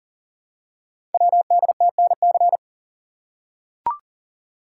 Morse code, heard as a keyed tone, about 40 wpm, 700 Hz. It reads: WDTNC
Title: Code Groups - 5 Letters - 40wpm